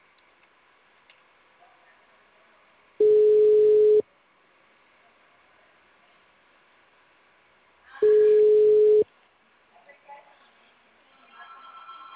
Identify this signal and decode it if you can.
Ringback tone (Europe, 425 Hz).